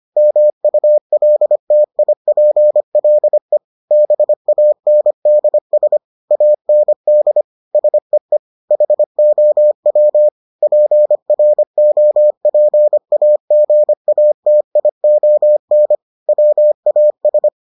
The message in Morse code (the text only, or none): MULTIPLE BANDS AND SEE HOW PROPAGATION WAS